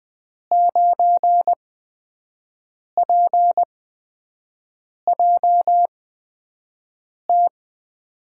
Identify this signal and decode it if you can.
Morse code, heard as a keyed tone, about 20 wpm, spaced as for 5 wpm, 700 Hz. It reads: 9PJT
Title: USA Call Signs - 6 Characters - 8x Character Spacing - 20wpm